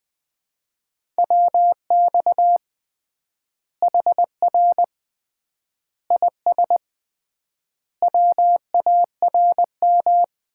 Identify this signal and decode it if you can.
Morse code, heard as a keyed tone, about 20 wpm, 700 Hz. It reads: WX HR IS WARM
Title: Common QSO Phrases - 3x Word-Spacing - 20wpm